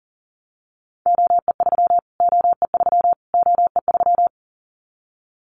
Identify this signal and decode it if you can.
Morse code, heard as a keyed tone, about 40 wpm, 700 Hz. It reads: OE3 OE3 OE3
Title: Callsign Prefixes for Top 25 DXCC - Repeated 3x - 40wpm